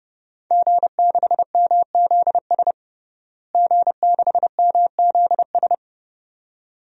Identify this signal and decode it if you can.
Morse code, heard as a keyed tone, about 30 wpm, 700 Hz. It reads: G6MZS G6MZS
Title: DXCC-100 Call Signs - 5 Characters - Repeated 2x - 30wpm